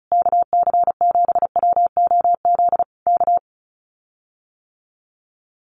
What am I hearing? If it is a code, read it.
Morse code, heard as a keyed tone, about 34 wpm, 700 Hz. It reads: KC7WOZ K
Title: Common QSO Phrases Encore 35wpm